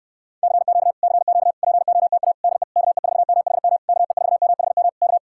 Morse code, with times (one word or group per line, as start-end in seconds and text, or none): 0.4-0.9 CQ
1.0-1.5 CQ
1.6-2.3 POTA
2.4-2.6 DE
2.8-3.8 KE4MHM
3.9-4.9 KE4MHM
5.0-5.2 K